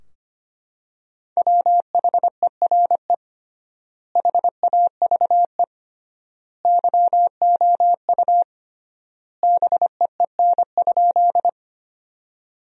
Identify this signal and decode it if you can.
Morse code, heard as a keyed tone, about 25 wpm, 700 Hz. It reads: WHERE HAVE YOU BEEN?
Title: 3x Word-Spacing - Sentences from Top 200 Words - 25wpm